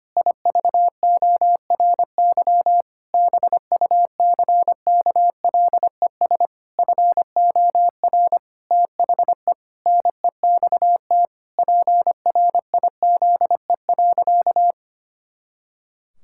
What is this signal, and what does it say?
Morse code, heard as a keyed tone, about 25 wpm, 700 Hz. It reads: IVORY BUCKLES FOR THE NEXT PRIZE.
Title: Pangram Challenge v2 - 25wpm